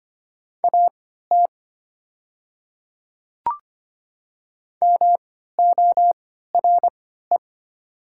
Morse code, heard as a keyed tone, about 25 wpm, spaced as for 14 wpm, 700 Hz. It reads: AT MORE
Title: 3x Character Spacing - Top 100 Words - 25wpm